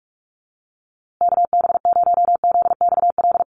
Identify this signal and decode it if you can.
Morse code, heard as a keyed tone, about 44 wpm, 700 Hz. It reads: KB0ZXL
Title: USA Call Signs Repeated 2x 45wpm